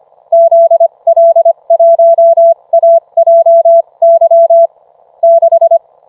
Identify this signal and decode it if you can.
Morse code, heard as a keyed tone, about 25 wpm, spaced as for 22 wpm, 670 Hz. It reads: ZL1AJY 6